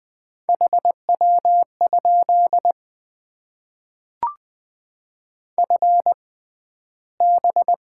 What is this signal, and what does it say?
Morse code, heard as a keyed tone, about 20 wpm, spaced as for 12 wpm, 700 Hz. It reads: HW? FB
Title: Top 20 QSO Elements - 6x Character Spacing - 20wpm